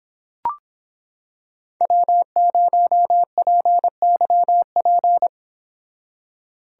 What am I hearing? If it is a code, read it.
Morse code, heard as a keyed tone, about 26 wpm, 700 Hz. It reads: W0PYP